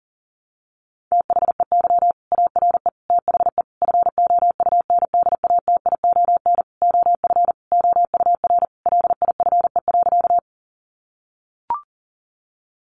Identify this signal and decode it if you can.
Morse code, heard as a keyed tone, about 40 wpm, 700 Hz. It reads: THEY ARE THE FOUNDATION OF OUR LIFE.
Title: Sentences from Top 1200 Words 40wpm